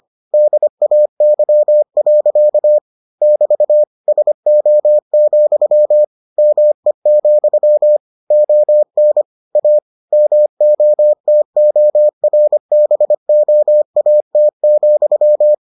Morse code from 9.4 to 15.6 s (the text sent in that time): A MOTORBOAT,